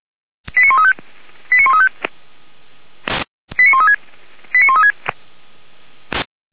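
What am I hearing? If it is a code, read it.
Five-tone selective call (ZVEI), heard as quick runs of five tones, tones of about 70 ms. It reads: 89135 89135 89136 89136
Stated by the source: Five-tone ZVEI selective call 89110 received by SDR.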